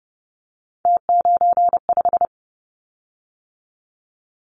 Morse code, heard as a keyed tone, about 30 wpm, 700 Hz. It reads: T95
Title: Callsign Prefixes for Top 50 DXCC - 30wpm